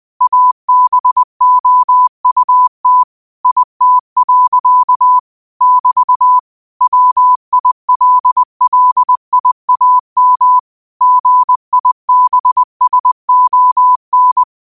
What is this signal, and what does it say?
Morse code, heard as a keyed tone, about 20 wpm, 1000 Hz. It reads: ABOUT IT. <BT> WILLIAM GIBSON
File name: qotdfiles / 20wpm